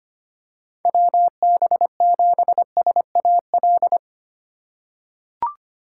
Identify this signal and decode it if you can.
Morse code, heard as a keyed tone, about 25 wpm, 700 Hz. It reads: WB7SAL
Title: USA Call Signs - 6 Characters - 8x Character Spacing - 25wpm